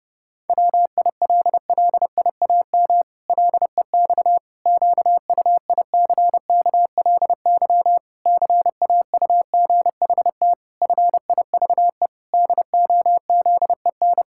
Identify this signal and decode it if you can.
Morse code, heard as a keyed tone, about 32 wpm, 700 Hz. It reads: WILLIAM LEX QUICKLY CAUGHT FIVE DOZEN
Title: Pangram Challenge v2 - 30wpm